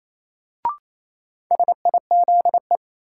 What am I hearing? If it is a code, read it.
Morse code, heard as a keyed tone, about 28 wpm, 700 Hz. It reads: SIZE
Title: Sentences from Top 400 Words - New Word Drill - 28wpm